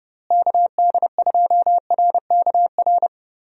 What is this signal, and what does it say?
Morse code, heard as a keyed tone, about 31 wpm, 700 Hz. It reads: KD2RKR
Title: USA Call Signs - 6 Characters - Repeated 2x - 30wpm